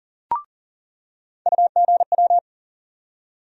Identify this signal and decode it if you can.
Morse code, heard as a keyed tone, about 40 wpm, 700 Hz. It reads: UGW
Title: USA Call Sign Postfixes - 3 Letters 40wpm